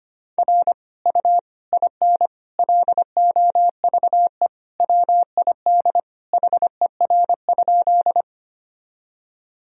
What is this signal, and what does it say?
Morse code, heard as a keyed tone, about 25 wpm, 700 Hz. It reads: R U IN LOVE WID HER?
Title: Sentences from Top 400 Words and Abbreviations 25wpm